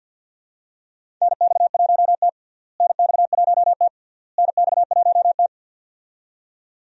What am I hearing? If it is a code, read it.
Morse code, heard as a keyed tone, about 53 wpm, 700 Hz. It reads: NX1T NX1T NX1T
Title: USA Call Signs - 4 Characters - Repeated 3x - 50wpm